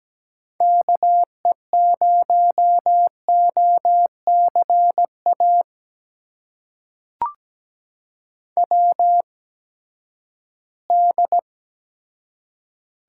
Morse code, heard as a keyed tone, about 17 wpm, spaced as for 12 wpm, 700 Hz. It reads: KE0OCA WD
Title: USA Call Signs - 6 Characters - 8x Character Spacing - 17wpm